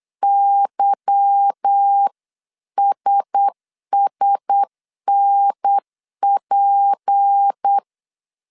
Morse code, heard as a keyed tone, about 8 wpm, 800 Hz. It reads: YSSNP
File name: CW / DET / 1 9 1500C S C 8wpm 5653